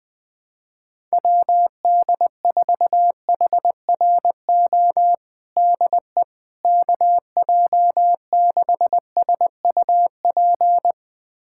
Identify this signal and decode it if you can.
Morse code, heard as a keyed tone, about 20 wpm, 700 Hz. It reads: WD4HRO DE KJ6SUP
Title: USA Callsign DE Callsign 20wpm